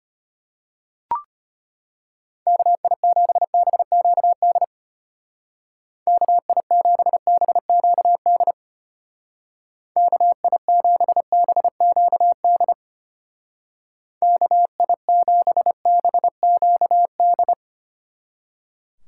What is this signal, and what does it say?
Morse code, heard as a keyed tone, about 33 wpm, 700 Hz. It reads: KI7BQD KI7BQD KI7BQD KI7BQD